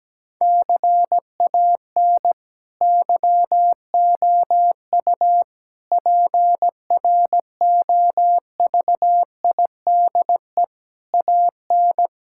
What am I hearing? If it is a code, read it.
Morse code, heard as a keyed tone, about 17 wpm, 700 Hz. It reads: CAN YOU PROVIDE AN
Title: Sentences from Top 1000 Words - Review Part 5 - 17wpm